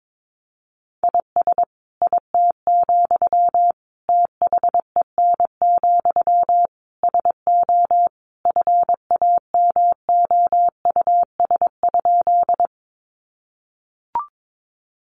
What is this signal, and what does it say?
Morse code, heard as a keyed tone, about 22 wpm, 700 Hz. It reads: IS IT, THEN, SO FAMOUS?
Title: Sentences from Top 1500 Words 22wpm